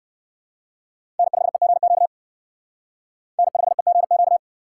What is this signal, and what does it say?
Morse code, heard as a keyed tone, about 59 wpm, 700 Hz. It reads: N5EGQ N5EGQ